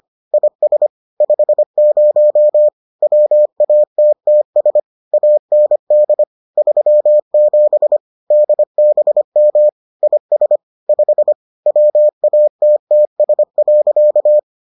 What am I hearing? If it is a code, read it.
Morse code, heard as a keyed tone, about 25 wpm, 600 Hz. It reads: IS 50 WATTS AND 37 DBM IS 5 WATTS.